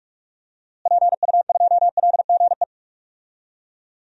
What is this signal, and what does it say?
Morse code, heard as a keyed tone, about 46 wpm, 700 Hz. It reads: PU2LGE
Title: DXCC-100 Call Signs - 6 Characters - Repeated 2x - 45wpm